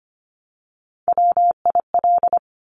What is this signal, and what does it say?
Morse code, heard as a keyed tone, about 25 wpm, 700 Hz. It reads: WIL
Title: Top Three Letter Combinations in Words 25wpm